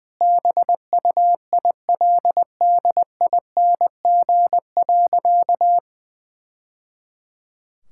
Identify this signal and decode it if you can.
Morse code, heard as a keyed tone, about 20 wpm, 700 Hz. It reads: BUILDING.
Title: Sentences from Top 1000 Words - Review Part 5 - 20wpm